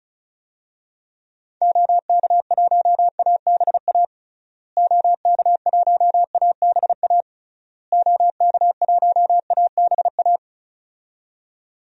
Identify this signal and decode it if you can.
Morse code, heard as a keyed tone, about 36 wpm, 700 Hz. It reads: OK1ABA OK1ABA OK1ABA
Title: DXCC-100 Call Signs - 6 Characters - Repeated 3x - 35wpm